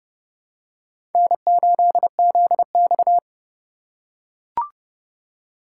Morse code, heard as a keyed tone, about 31 wpm, 700 Hz. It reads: N8ZX